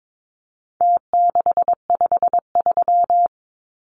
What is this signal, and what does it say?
Morse code, heard as a keyed tone, about 22 wpm, 700 Hz. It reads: T653
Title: Numbers - Contest Style - 4-Digit 22wpm